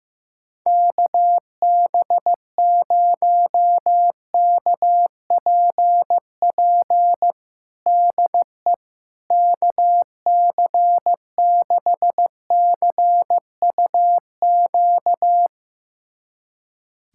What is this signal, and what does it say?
Morse code, heard as a keyed tone, about 15 wpm, 700 Hz. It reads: KB0KPP DE KC6CUQ
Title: USA Callsign DE Callsign 15wpm